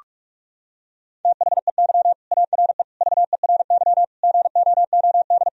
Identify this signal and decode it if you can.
Morse code, heard as a keyed tone, about 42 wpm, 700 Hz. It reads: THEY ARE VERY GOOD
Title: Sentences from Top 1000 Words - Review Part 3 - 45wpm